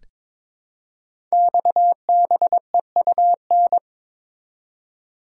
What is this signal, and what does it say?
Morse code, heard as a keyed tone, about 22 wpm, 700 Hz. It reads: XBEUN
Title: Code Groups - 5 Letters - 22wpm